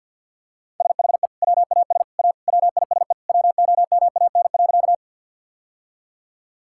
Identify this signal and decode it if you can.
Morse code, heard as a keyed tone, about 50 wpm, 700 Hz. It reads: SHE WAS A WISE WOMAN.
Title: Sentences from Top 1600 Words 50wpm